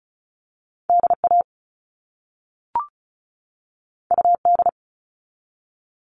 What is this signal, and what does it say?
Morse code, heard as a keyed tone, about 36 wpm, 700 Hz. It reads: DA UD
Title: All Cut Numbers - 2-Digit 35wpm